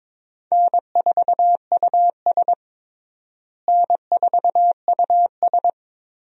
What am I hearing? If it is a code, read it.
Morse code, heard as a keyed tone, about 21 wpm, 700 Hz. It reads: N4US N4US
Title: USA Call Signs - 4 Characters - Repeated 3x - 22wpm